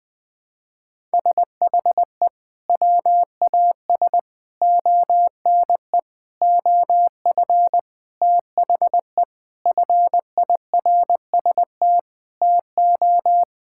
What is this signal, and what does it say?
Morse code, heard as a keyed tone, about 20 wpm, 700 Hz. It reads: SHE WAS ONE OF THE FIRST TO
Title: Sentences with 4-Word Phrases - Spoken Beginning - 20wpm